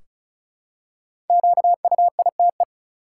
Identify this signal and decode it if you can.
Morse code, heard as a keyed tone, about 36 wpm, 700 Hz. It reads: QUITE
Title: Five Letter Words - 35wpm